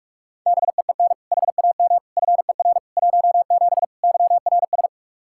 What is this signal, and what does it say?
Morse code, heard as a keyed tone, about 46 wpm, 700 Hz. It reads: BEEN HAM FER 17 YRS
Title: K1USN SST QSO Practice 45wpm